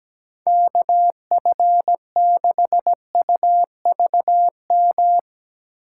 Morse code, heard as a keyed tone, about 17 wpm, 700 Hz. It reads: K F 6 U V M
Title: USA Call Signs - 6 Characters - 8x Character Spacing - 17wpm